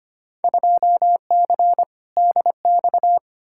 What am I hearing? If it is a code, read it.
Morse code, heard as a keyed tone, about 25 wpm, 700 Hz. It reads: 2C DX
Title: Field Day Exchanges Encore - 25wpm